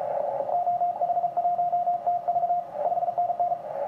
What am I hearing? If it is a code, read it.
Morse code, heard as a keyed tone, about 34 wpm, 700 Hz, with radio noise. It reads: OH1TV 5NN
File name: ti9a 80cw